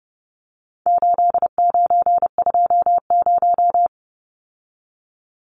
Morse code, heard as a keyed tone, about 30 wpm, 700 Hz. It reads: 8920
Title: Numbers - 4-Digit 30wpm